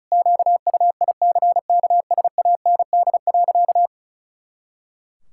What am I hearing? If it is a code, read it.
Morse code, heard as a keyed tone, about 34 wpm, 700 Hz. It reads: QUICKSAND.
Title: Pangram Challenge v2 - 35wpm